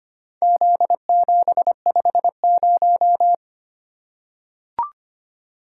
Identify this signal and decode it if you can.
Morse code, heard as a keyed tone, about 25 wpm, 700 Hz. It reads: Z750
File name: practice / 25wpm / Four Letter-Number Combinations - 25wpm